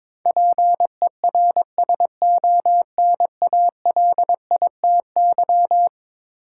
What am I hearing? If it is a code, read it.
Morse code, heard as a keyed tone, about 22 wpm, 700 Hz. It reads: PERSONALITY
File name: practice / 22wpm / Eleven Letter Words - 22wpm